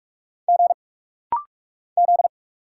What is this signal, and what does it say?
Morse code, heard as a keyed tone, about 46 wpm, 700 Hz. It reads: G Z
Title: Single Letters - Rapid-Fire 45wpm